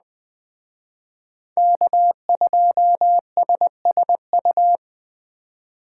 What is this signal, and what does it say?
Morse code, heard as a keyed tone, about 20 wpm, 700 Hz. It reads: K2SSU
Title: Speed Racing - USA 1x3 Call Signs 17wpm